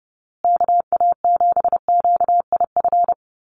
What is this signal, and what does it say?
Morse code, heard as a keyed tone, about 30 wpm, 700 Hz. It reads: KA7QIF
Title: USA Call Signs Repeated 2x 30wpm